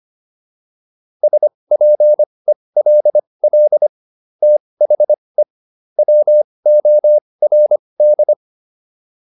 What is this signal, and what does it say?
Morse code, heard as a keyed tone, about 25 wpm, spaced as for 20 wpm, 600 Hz. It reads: SPELL THE WORD